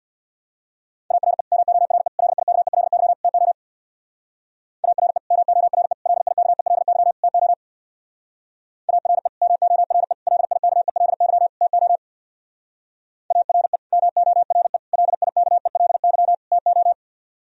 Morse code, heard as a keyed tone, about 64 wpm, 700 Hz. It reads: ARE MORE LIKELY TO ARE MORE LIKELY TO ARE MORE LIKELY TO ARE MORE LIKELY TO